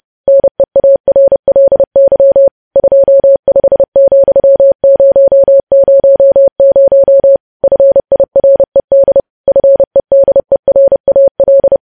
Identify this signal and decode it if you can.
Morse code, heard as a keyed tone, about 30 wpm, 550 Hz. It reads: NEARLY 25,000 FIRED FEDERAL
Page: news headlines in morse code at 30 wpm